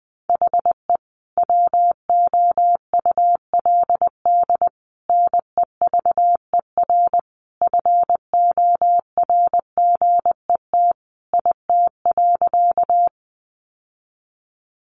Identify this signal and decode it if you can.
Morse code, heard as a keyed tone, about 20 wpm, 700 Hz. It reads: HE WOULD NEVER FORGET IT.
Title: Sentences from Top 1200 Words 20wpm